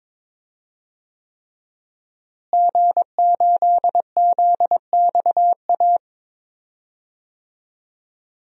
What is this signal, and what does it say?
Morse code, heard as a keyed tone, about 22 wpm, 700 Hz. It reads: G8ZXA
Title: DXCC-100 Call Signs - 5 Characters - Repeated 3x - 22wpm